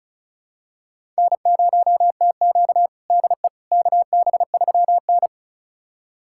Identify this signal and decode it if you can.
Morse code, heard as a keyed tone, about 36 wpm, 700 Hz. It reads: N0TQ DE KB3N
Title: USA Callsign DE Callsign - 4 Characters - 35wpm